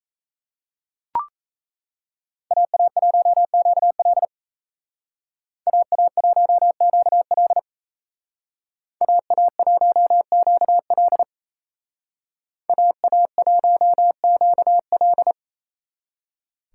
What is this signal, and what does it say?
Morse code, heard as a keyed tone, about 37 wpm, 700 Hz. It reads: AA1QL AA1QL AA1QL AA1QL